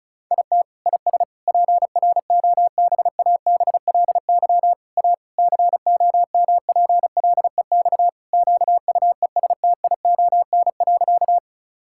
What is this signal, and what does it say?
Morse code, heard as a keyed tone, about 36 wpm, 700 Hz. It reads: IT IS PROBABLY A COMPLEX QUESTION.
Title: Sentences from Top 1000 Words - Review Part 6 - 35wpm